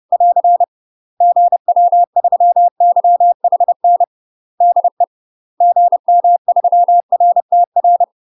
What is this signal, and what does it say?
Morse code, heard as a keyed tone, about 30 wpm, 700 Hz. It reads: <AR> GW3YHN DE GM3RTR
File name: UK MASTER QSOs TRAINING / 30WPM / UK MASTER QSOs 0052